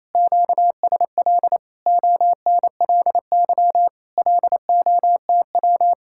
Morse code, heard as a keyed tone, about 37 wpm, 700 Hz. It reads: QSL ONLY LOTW